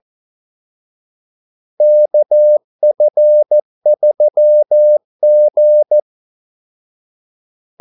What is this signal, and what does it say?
Morse code, heard as a keyed tone, about 14 wpm, 600 Hz. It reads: KF3G
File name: Callsigns-14wpm